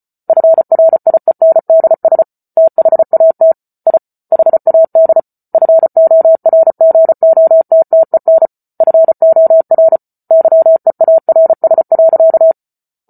Morse code, heard as a keyed tone, about 34 wpm, 650 Hz. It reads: FRIENDS THAT I HAD FORGOTTEN FOR YEARS.